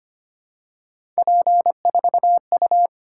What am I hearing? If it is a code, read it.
Morse code, heard as a keyed tone, about 25 wpm, 700 Hz. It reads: P4U
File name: Three Letter-Number Combinations - 25wpm